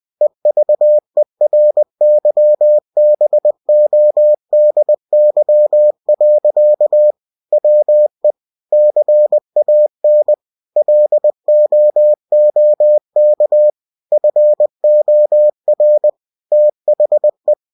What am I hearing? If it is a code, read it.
Morse code, heard as a keyed tone, about 20 wpm, 600 Hz. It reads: EVERYBODY. WE CAN LOOK FOR THE
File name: cw-books / winnie-the-pooh / winnie-20wpm-page-0080